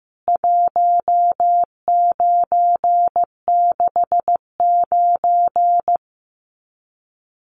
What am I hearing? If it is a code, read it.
Morse code, heard as a keyed tone, about 15 wpm, 700 Hz. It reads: 1969
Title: CWT Exchange 15wpm